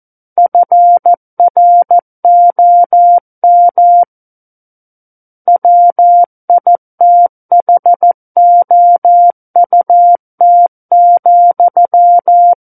Morse code, heard as a keyed tone, about 14 wpm, 700 Hz. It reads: FROM WITHOUT,
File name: Book One / 05-Chapter 5 - 14 WPM